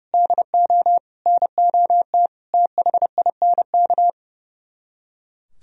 Morse code, heard as a keyed tone, about 40 wpm, 700 Hz. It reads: DO NOT THINK